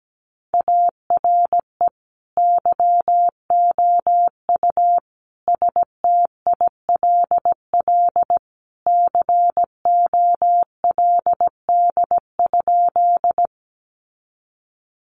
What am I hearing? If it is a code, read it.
Morse code, heard as a keyed tone, about 17 wpm, 700 Hz. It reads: ARE YOU STILL COLD?
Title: Sentences from Top 400 Words 17wpm